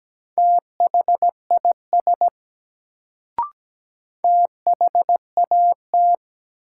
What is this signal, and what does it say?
Morse code, heard as a keyed tone, about 17 wpm, 700 Hz. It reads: THIS THAT
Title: Sets of 2 Words - New Word Drill - 17wpm